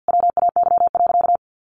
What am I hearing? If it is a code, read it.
Morse code, heard as a keyed tone, about 50 wpm, 700 Hz. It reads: WAY.
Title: Sentences from Top 600 Words 50wpm